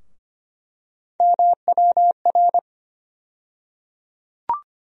Morse code, heard as a keyed tone, about 25 wpm, 700 Hz. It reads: MWR
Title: USA Call Sign Postfixes - 3 Letters 25wpm